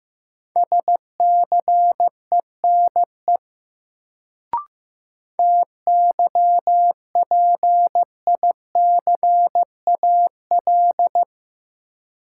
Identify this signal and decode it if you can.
Morse code, heard as a keyed tone, about 15 wpm, 700 Hz. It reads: SCENE TYPICAL
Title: Sentences from Top 1100 Words - New Word Drill - 15wpm